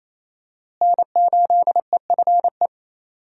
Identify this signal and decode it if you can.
Morse code, heard as a keyed tone, about 28 wpm, 700 Hz. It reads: N8EFE
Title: USA Call Signs - 5 Characters - Repeated 3x - 28wpm